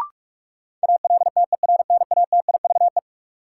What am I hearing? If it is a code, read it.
Morse code, heard as a keyed tone, about 43 wpm, 700 Hz. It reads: ALTERNATIVE
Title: Sentences from Top 1000 Words - Review Part 4 - New Word Drill - 45wpm